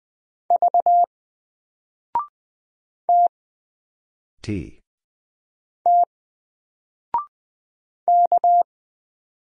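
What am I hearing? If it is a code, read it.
Morse code, heard as a keyed tone, about 20 wpm, 700 Hz. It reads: V T T K
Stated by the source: Single letter and number Morse drill at 20 wpm, from N4LCD.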